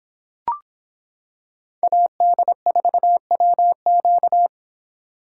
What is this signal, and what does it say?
Morse code, heard as a keyed tone, about 26 wpm, 700 Hz. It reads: AD4WQ